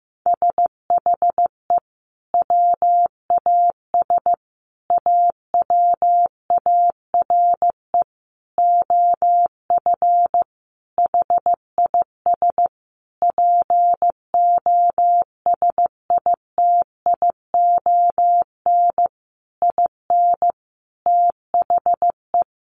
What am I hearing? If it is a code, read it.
Morse code, heard as a keyed tone, about 15 wpm, 700 Hz. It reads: SHE WAS AWARE OF HIS POSITION IN THE
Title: Sentences from Top 1200 Words 15wpm